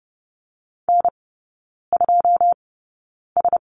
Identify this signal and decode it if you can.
Morse code, heard as a keyed tone, about 30 wpm, spaced as for 9 wpm, 700 Hz. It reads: N2S
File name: Lesson 19e - Letter w - Three Characters 30wpm